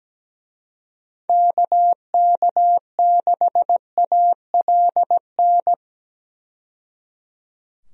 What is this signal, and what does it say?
Morse code, heard as a keyed tone, about 17 wpm, 700 Hz. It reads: KK6ALN
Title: USA Call Signs - 6 Characters - 17wpm